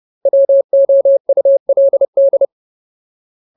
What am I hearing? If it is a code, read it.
Morse code, heard as a keyed tone, about 30 wpm, 550 Hz. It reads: WOULD
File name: Ultimate Word List / 30WPM Ultimate